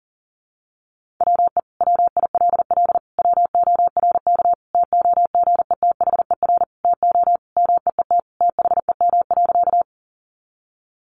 Morse code, heard as a keyed tone, about 40 wpm, 700 Hz. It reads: WE WILL WORK TOGETHER TO MEET THEM.
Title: Sentences from Top 1100 Words 40wpm